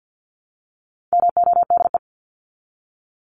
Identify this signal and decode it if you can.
Morse code, heard as a keyed tone, about 48 wpm, 700 Hz. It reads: MODE
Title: Top 1751-2000 Words 50wpm